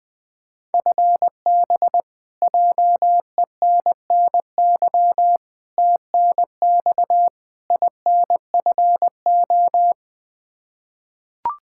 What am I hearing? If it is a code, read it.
Morse code, heard as a keyed tone, about 20 wpm, 700 Hz. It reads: FB JENNY TNX INFO
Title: K1USN SST QSO Practice 20wpm